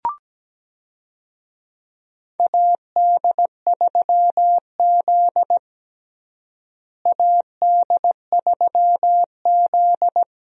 Morse code, heard as a keyed tone, about 17 wpm, 700 Hz. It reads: AD3Z AD3Z
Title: USA Call Signs - 4 Characters - Repeated 2x - 17wpm